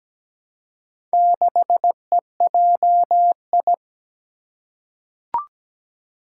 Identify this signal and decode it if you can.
Morse code, heard as a keyed tone, about 17 wpm, 700 Hz. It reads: 6EJI
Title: Four Challenging Letter-Number Combinations 17wpm